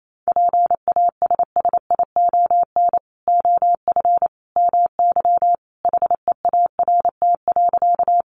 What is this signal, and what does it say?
Morse code, heard as a keyed tone, about 28 wpm, 700 Hz. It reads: PASSION OF MY HEART.
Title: Sentences from Top 1300 Words 28wpm